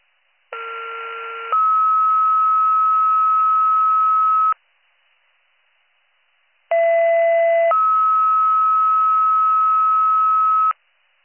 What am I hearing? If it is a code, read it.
Two-tone page, heard as a one-second tone, then a three-second tone, about 1370 then 1290 Hz, about 670 then 1290 Hz. It is heard from 0.5 to 10.7 s.